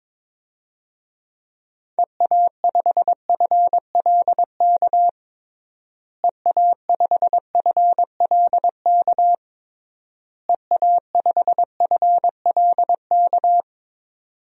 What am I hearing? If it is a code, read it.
Morse code, heard as a keyed tone, about 22 wpm, 700 Hz. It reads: EA5FLK EA5FLK EA5FLK
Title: DXCC-100 Call Signs - 6 Characters - Repeated 3x - 22wpm